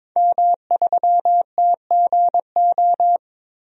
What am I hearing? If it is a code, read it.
Morse code, heard as a keyed tone, about 22 wpm, 700 Hz. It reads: M3TGO
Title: DXCC-100 Call Signs - 5 Characters - 22wpm